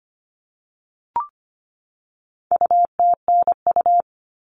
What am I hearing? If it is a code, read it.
Morse code, heard as a keyed tone, about 25 wpm, 700 Hz. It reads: UTNU
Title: All Cut Numbers - 4-Digit 25wpm